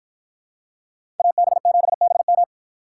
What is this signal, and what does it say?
Morse code, heard as a keyed tone, about 52 wpm, 700 Hz. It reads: AB7BM